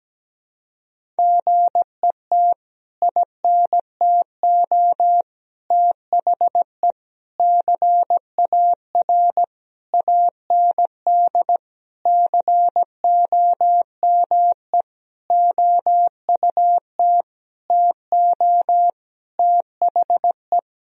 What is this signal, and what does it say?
Morse code, heard as a keyed tone, about 17 wpm, 700 Hz. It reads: GET INTO THE CAR AND COME OUT TO THE
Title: Sentences from Top 1000 Words - Review Part 1 - 17wpm